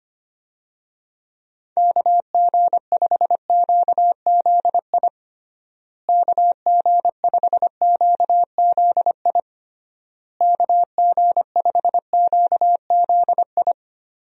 Morse code, heard as a keyed tone, about 25 wpm, 700 Hz. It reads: KG5QZI KG5QZI KG5QZI
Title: USA Call Signs - 6 Characters - Repeated 3x - 25wpm